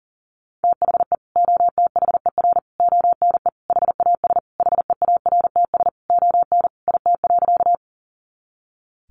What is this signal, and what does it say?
Morse code, heard as a keyed tone, about 40 wpm, 700 Hz. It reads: THE OTHER ONE HAS HEARTS ON IT.
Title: Additional Easy Phrases 40wpm